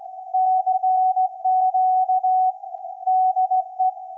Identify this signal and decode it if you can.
Morse code, heard as a keyed tone, about 17 wpm, 750 Hz. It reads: CQ DE
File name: iz7slz / VLF / SAQ 24-12-2024